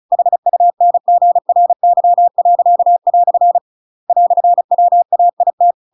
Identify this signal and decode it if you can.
Morse code, heard as a keyed tone, about 34 wpm, 700 Hz. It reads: HUNGRY." "WAIT